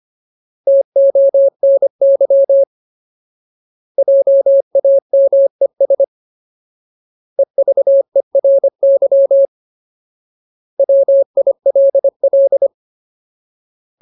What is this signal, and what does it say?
Morse code, heard as a keyed tone, about 25 wpm, 550 Hz. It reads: TONY JAMES EVERY WILL
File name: Ultimate Word List / 25WPM Ultimate